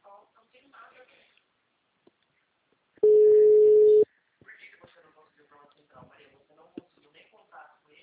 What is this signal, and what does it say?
Ringback tone (Europe, 425 Hz), with speech.